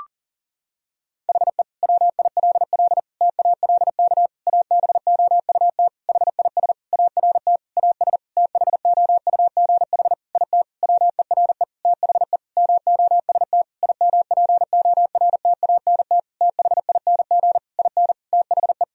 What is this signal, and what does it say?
Morse code, heard as a keyed tone, about 37 wpm, 700 Hz. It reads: HE WILL TALK ABOUT HIS ART AS THOUGH IT WERE THE MOST IMPORTANT THING IN THE
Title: Sentences from Top 1000 Words - Review Part 3 - 40wpm